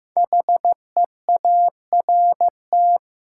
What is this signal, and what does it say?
Morse code, heard as a keyed tone, about 15 wpm, 700 Hz. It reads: HEART
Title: Sentences from Top 1200 Words - New Word Drill Alt - 15wpm